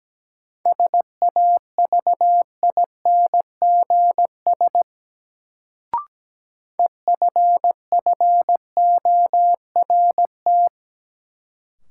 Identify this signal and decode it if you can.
Morse code, heard as a keyed tone, about 17 wpm, 700 Hz. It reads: SAVINGS EFFORT
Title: Sentences from Top 1200 Words - New Word Drill - 17wpm